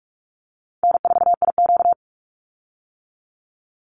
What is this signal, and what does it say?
Morse code, heard as a keyed tone, about 44 wpm, 700 Hz. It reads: N4IQ
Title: Easy Call Signs - 45wpm